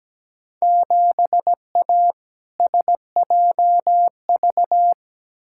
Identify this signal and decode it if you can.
Morse code, heard as a keyed tone, about 17 wpm, 700 Hz. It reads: 7A SJV
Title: Field Day Exchanges Encore - 17wpm